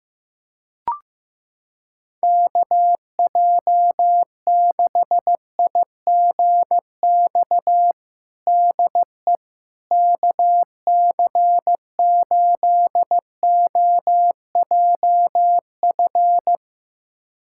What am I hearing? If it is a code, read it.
Morse code, heard as a keyed tone, about 15 wpm, 700 Hz. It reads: KJ6IGX DE KC8OJF
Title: USA Callsign DE Callsign - 6 Characters - 15wpm